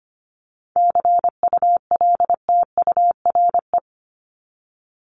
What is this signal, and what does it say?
Morse code, heard as a keyed tone, about 25 wpm, 700 Hz. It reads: CULTURE
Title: Top 751-1000 Words 25wpm